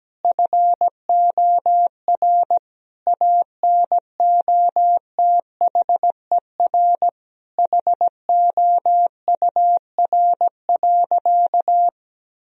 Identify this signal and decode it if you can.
Morse code, heard as a keyed tone, about 17 wpm, 700 Hz. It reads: FOR ANOTHER HOUR.
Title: Sentences from Top 1000 Words - Review Part 4 - 17wpm